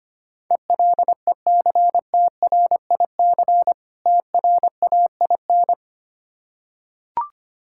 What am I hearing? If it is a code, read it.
Morse code, heard as a keyed tone, about 25 wpm, 690 Hz. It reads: ELECTRIC TRAIN
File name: practice / 25wpm / Difficult Words with Hints - 25wpm